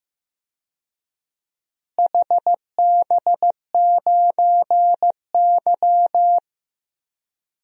Morse code, heard as a keyed tone, about 15 wpm, 700 Hz. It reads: HB9Y
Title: DXCC-100 Call Signs - 4 Characters - Repeated 3x - 15wpm